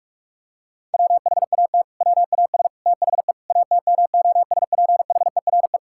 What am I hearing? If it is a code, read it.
Morse code, heard as a keyed tone, about 48 wpm, 700 Hz. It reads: WHAT WAS THE ATMOSPHERE
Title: Sentences from Top 1600 Words 45wpm